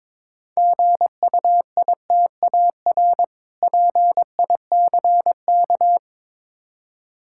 Morse code, heard as a keyed tone, about 22 wpm, 700 Hz. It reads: GUITAR PICK
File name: Easy Words - Speed-Racing with Hints - 22wpm